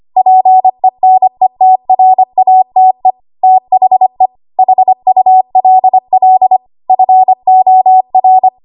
Morse code, heard as a keyed tone, about 25 wpm, 750 Hz. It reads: PENETRATE THE HULL FOR
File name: Archive / 25 WPM / 25WPM